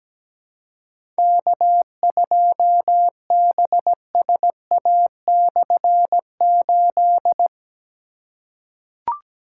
Morse code, heard as a keyed tone, about 17 wpm, 700 Hz. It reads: K2BSA/8
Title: K1USN SST QSO Practice - 17wpm